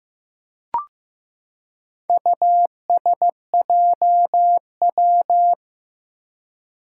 Morse code, heard as a keyed tone, about 15 wpm, 700 Hz. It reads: USJW
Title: Four Challenging Letter-Number Combinations 15wpm